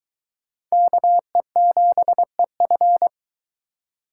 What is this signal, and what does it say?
Morse code, heard as a keyed tone, about 22 wpm, 700 Hz. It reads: KE7EF